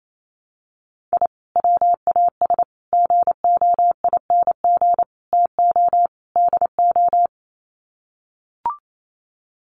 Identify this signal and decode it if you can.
Morse code, heard as a keyed tone, about 28 wpm, 700 Hz. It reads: I WAS GOING TO DO
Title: Sets of 5 Words Encore2 28wpm